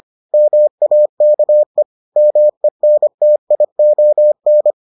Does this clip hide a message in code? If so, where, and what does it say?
Yes: Morse code from 0.3 to 4.7 s, heard as a keyed tone, about 25 wpm, 600 Hz. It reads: MAKE MENTION